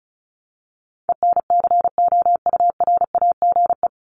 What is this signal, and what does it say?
Morse code, heard as a keyed tone, about 34 wpm, 700 Hz. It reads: ENCOURAGE
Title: Top 1251-1500 Words 35wpm